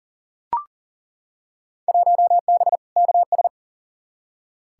Morse code, heard as a keyed tone, about 40 wpm, 700 Hz. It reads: 1B KS
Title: Field Day Exchanges Encore - 40wpm